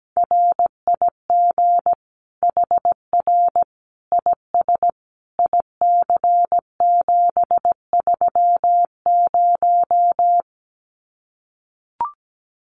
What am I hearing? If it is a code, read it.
Morse code, heard as a keyed tone, about 17 wpm, 700 Hz. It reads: RIG HR IS IC730
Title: Common QSO Phrases 17wpm